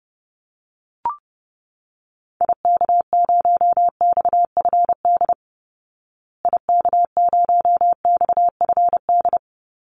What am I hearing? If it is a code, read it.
Morse code, heard as a keyed tone, about 30 wpm, 700 Hz. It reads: IK0XFD IK0XFD
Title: Call Signs Repeated 3x - 30wpm